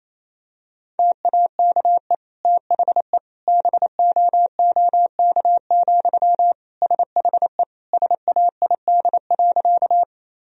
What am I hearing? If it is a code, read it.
Morse code, heard as a keyed tone, about 28 wpm, 700 Hz. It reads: TAKE THE BOOK, SHE SAID.
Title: Sentences from Top 1000 Words - Review Part 1 - 28wpm